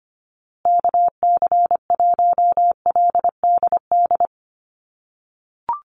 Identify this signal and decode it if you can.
Morse code, heard as a keyed tone, about 25 wpm, 700 Hz. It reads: KC1LDD
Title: USA Call Signs Repeated 2x 25wpm